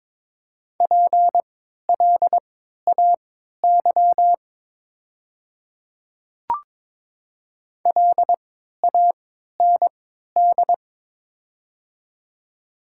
Morse code, heard as a keyed tone, about 22 wpm, spaced as for 13 wpm, 700 Hz. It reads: PLAY LAND
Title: 3x Character Spacing - Top 300 Words - 22wpm